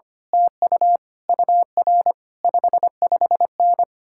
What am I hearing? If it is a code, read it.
Morse code, heard as a keyed tone, about 24 wpm, 700 Hz. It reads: TU UR 55N
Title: Speed-Racing Common QSO Phrases - 25wpm